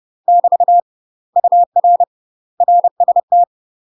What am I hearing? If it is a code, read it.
Morse code, heard as a keyed tone, about 30 wpm, 700 Hz. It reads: <BT> UR RST
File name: UK MASTER QSOs TRAINING / 30WPM / UK MASTER QSOs 0000